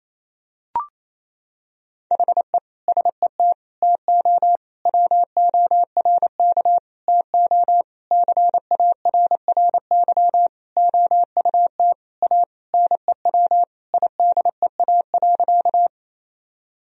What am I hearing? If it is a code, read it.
Morse code, heard as a keyed tone, about 28 wpm, 700 Hz. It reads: HE SET TO WORK TO CARRY OUT A NEW IDEA.
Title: Sentences from Top 1000 Words - Review Part 1 - 28wpm